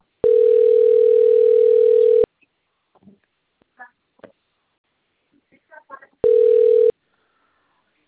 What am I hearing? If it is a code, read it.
Ringback tone (North America), 440 + 480 Hz, with speech.
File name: out- - - - - .9736